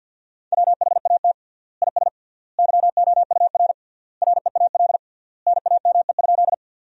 Morse code, heard as a keyed tone, about 50 wpm, 700 Hz. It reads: WHAT IS YOUR REAL NAME?
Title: 3x Word-Spacing - Sentences from Top 300 Words - 50wpm